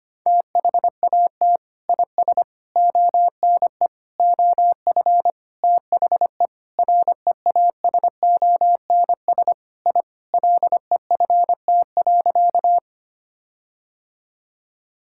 Morse code, heard as a keyed tone, about 25 wpm, 700 Hz. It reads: THAT IS ONE OF THE REASONS I LEFT.
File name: practice / 25wpm / Sentences with 4-Word Phrases - 25wpm